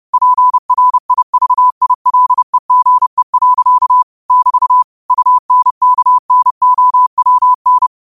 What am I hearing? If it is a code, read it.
Morse code, heard as a keyed tone, about 30 wpm, 1000 Hz. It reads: PRIVILEGE. <BT> UNKNOWN